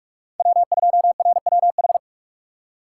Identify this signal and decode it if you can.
Morse code, heard as a keyed tone, about 45 wpm, 700 Hz. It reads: W2RWH
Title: USA Call Signs - 5 Characters - 4x Character Spacing - 45wpm